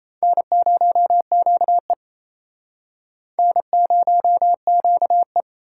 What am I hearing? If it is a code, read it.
Morse code, heard as a keyed tone, about 36 wpm, 700 Hz. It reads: N0QE N0QE